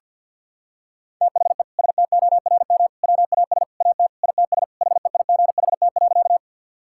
Morse code, heard as a keyed tone, about 50 wpm, 700 Hz. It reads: THE STORM WAS AT ITS HEIGHT.
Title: Sentences from Top 1600 Words 50wpm